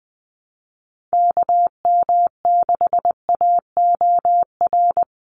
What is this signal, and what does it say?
Morse code, heard as a keyed tone, about 20 wpm, 700 Hz. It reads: KM6AOR
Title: USA Call Signs 20wpm